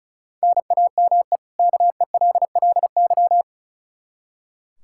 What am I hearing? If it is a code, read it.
Morse code, heard as a keyed tone, about 34 wpm, 700 Hz. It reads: NAME KELLY
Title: Practice QSO - Exchange 1 - Beginning - 35wpm